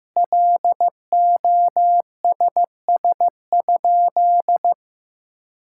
Morse code, heard as a keyed tone, about 15 wpm, 700 Hz. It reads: LOSS?
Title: Sentences from Top 1600 Words 15wpm